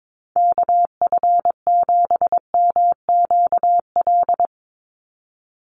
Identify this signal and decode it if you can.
Morse code, heard as a keyed tone, about 22 wpm, 700 Hz. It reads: KF7MQL
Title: USA Call Signs 22wpm